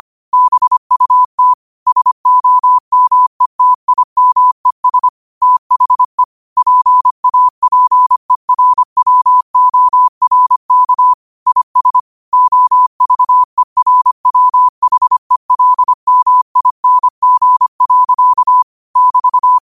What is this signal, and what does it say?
Morse code, heard as a keyed tone, about 25 wpm, 1000 Hz. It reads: BUT SOMETIMES THE PAPERWORK IS OVERWHELMING. <BT>